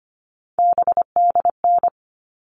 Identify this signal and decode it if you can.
Morse code, heard as a keyed tone, about 25 wpm, 700 Hz. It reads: BDN
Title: All Cut Numbers - 3-Digit 25wpm